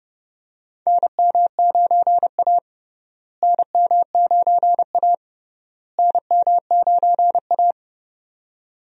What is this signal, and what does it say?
Morse code, heard as a keyed tone, about 30 wpm, 700 Hz. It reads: NM9A NM9A NM9A
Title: USA Call Signs - 4 Characters - Repeated 3x - 30wpm